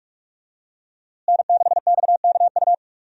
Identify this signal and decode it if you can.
Morse code, heard as a keyed tone, about 46 wpm, 700 Hz. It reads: N6XKU
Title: USA Call Signs - 5 Characters - Repeated 3x - 45wpm